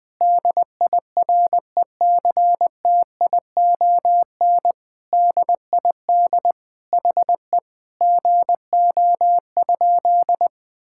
Morse code, heard as a keyed tone, about 21 wpm, 700 Hz. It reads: DIRECTION DID HE GO?
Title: Sentences from Top 1000 Words - Review Part 5 - 20wpm